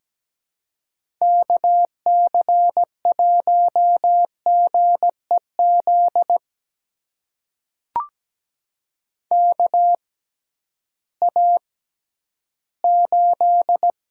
Morse code, heard as a keyed tone, about 17 wpm, spaced as for 11 wpm, 700 Hz. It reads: KC1GEZ KA8
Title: USA Call Signs - 6 Characters - 6x Character Spacing - 17wpm